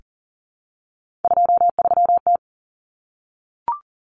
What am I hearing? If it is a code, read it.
Morse code, heard as a keyed tone, about 40 wpm, 700 Hz. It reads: 23T
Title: Common Cut Numbers - 3-Digit 40wpm